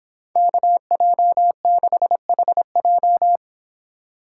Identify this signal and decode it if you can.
Morse code, heard as a keyed tone, about 26 wpm, 700 Hz. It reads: KJ6HJ